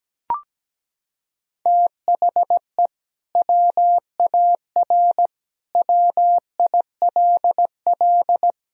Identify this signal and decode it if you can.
Morse code, heard as a keyed tone, about 17 wpm, 700 Hz. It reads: THE WAR WILL
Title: Sentences from Top 1000 Words - Review Part 2 - 17wpm